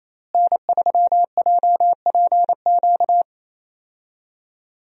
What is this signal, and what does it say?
Morse code, heard as a keyed tone, about 27 wpm, 700 Hz. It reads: N3JPQ
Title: USA Call Signs - 5 Characters - Repeated 2x - 28wpm